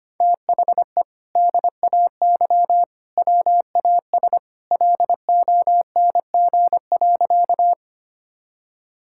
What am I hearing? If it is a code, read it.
Morse code, heard as a keyed tone, about 25 wpm, 700 Hz. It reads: THE DAY WAS LONG.
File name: practice / 25wpm / Sentences with 2-Word Phrases - 25wpm